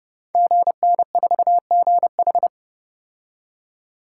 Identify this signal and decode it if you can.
Morse code, heard as a keyed tone, about 30 wpm, 700 Hz. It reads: GN4GH
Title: Code Groups - 5 Letter:Numbers - 30wpm